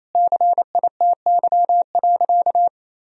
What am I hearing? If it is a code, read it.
Morse code, heard as a keyed tone, about 28 wpm, 700 Hz. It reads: CITY.
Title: Sentences from Top 1000 Words - Review Part 4 - 28wpm